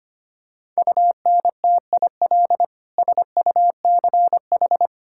Morse code, heard as a keyed tone, about 25 wpm, 700 Hz. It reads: UNTIL SUCH
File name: Sets of 9 Words - 25wpm